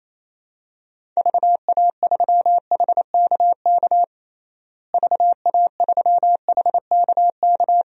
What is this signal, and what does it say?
Morse code, heard as a keyed tone, about 28 wpm, 700 Hz. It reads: VA3HKK VA3HKK
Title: DXCC-100 Call Signs - 6 Characters - Repeated 2x - 28wpm